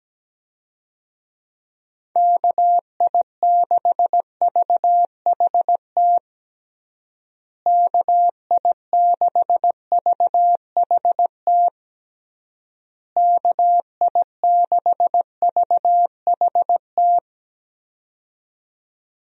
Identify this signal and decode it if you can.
Morse code, heard as a keyed tone, about 17 wpm, 700 Hz. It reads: KI6VHT KI6VHT KI6VHT
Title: USA Call Signs - 6 Characters - Repeated 3x - 17wpm